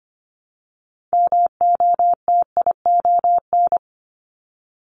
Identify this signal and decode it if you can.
Morse code, heard as a keyed tone, about 25 wpm, 700 Hz. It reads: MOTION
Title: Multi-letter Phonemes Repeated 3x 25wpm